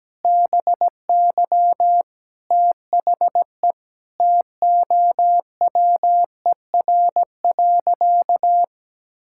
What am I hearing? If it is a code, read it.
Morse code, heard as a keyed tone, about 17 wpm, 700 Hz. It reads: BY THE TOWER.
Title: Pangram Challenge v2 - 17wpm